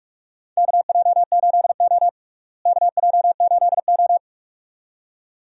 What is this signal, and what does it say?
Morse code, heard as a keyed tone, about 41 wpm, 700 Hz. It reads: KJ8O KJ8O
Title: USA Call Signs - 4 Characters - Repeated 2x - 45wpm